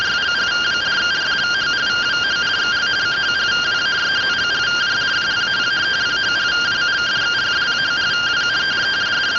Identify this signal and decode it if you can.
Two-tone teleprinter signal, about 170 Hz shift.